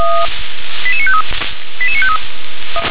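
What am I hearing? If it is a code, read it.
Five-tone selective call (ZVEI), heard as quick runs of five tones, tones of about 70 ms. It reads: 88953 88953
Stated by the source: ZVEI five-tone selective call received on an SDR.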